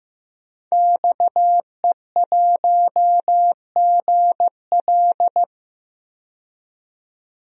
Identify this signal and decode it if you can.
Morse code, heard as a keyed tone, about 15 wpm, 700 Hz. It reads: XE1GL
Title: DXCC-100 Call Signs - 5 Characters - 15wpm